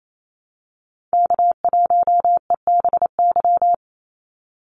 Morse code, heard as a keyed tone, about 28 wpm, 700 Hz. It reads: K1EBY
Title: Call Signs - 28wpm